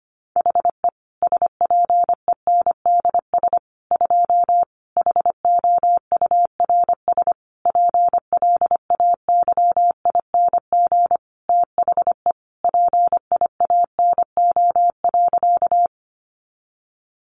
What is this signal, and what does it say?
Morse code, heard as a keyed tone, about 25 wpm, 700 Hz. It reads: HE SPENDS 2 HOURS PLAYING THE PIANO.
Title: Additional Easy Phrases 25wpm